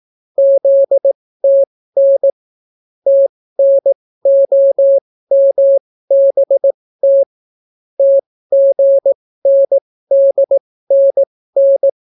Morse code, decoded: ZTN TNOMBT TGNDNN

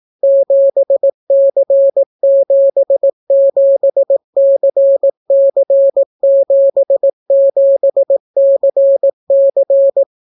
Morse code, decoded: 7C77CC77CC

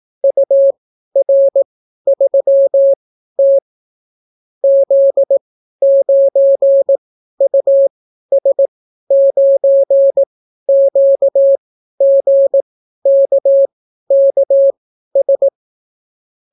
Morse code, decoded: UR3T Z9US9QGKKS

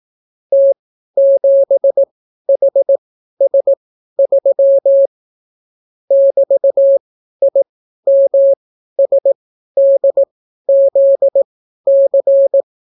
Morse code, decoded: T7HS3 <BT>IMSDZC

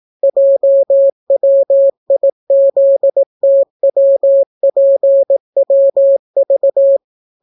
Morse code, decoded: JWIZTWPWV

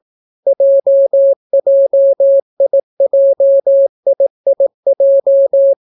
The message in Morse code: JJIJIIJ

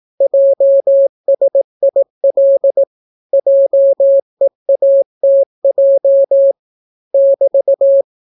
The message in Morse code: JSIL JEATJ <BT>

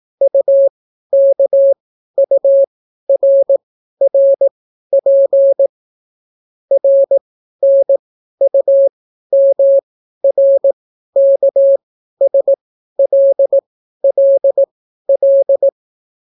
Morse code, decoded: UKURRP RNUMRKSLLL

